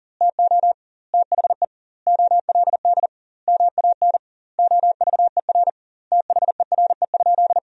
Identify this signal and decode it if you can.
Morse code, heard as a keyed tone, about 40 wpm, 700 Hz. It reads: TO THE OLD MAN OVER THERE?
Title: 2x Word-Spacing - Sentences from Top 200 Words - 40wpm